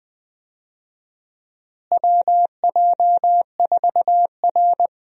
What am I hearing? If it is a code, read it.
Morse code, heard as a keyed tone, about 20 wpm, 700 Hz. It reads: WJ4R
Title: USA Call Signs - 4 Characters - Repeated 3x - 20wpm